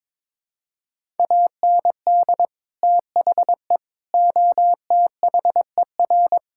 Morse code, heard as a keyed tone, about 21 wpm, 700 Hz. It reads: AND THE OTHER
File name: practice / 22wpm / Sentences with 3-Word Phrases - 22wpm